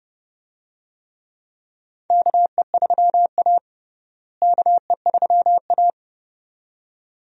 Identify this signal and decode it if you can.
Morse code, heard as a keyed tone, about 30 wpm, 700 Hz. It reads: KE3A KE3A
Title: USA Call Signs - 4 Characters - Repeated 2x - 30wpm